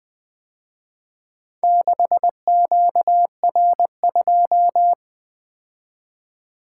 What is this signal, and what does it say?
Morse code, heard as a keyed tone, about 24 wpm, 700 Hz. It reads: 6QR2
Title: Speed-Racing - Four Letter-Number Combinations - 20wpm